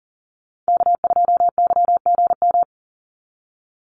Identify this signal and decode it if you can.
Morse code, heard as a keyed tone, about 40 wpm, 700 Hz. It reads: K2YGM
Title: Call Signs Repeated 2x - 40wpm